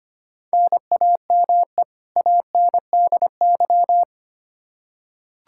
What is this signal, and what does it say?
Morse code, heard as a keyed tone, about 25 wpm, 700 Hz. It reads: NAME ANDY
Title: Practice QSO - Exchange 1 - Beginning - 25wpm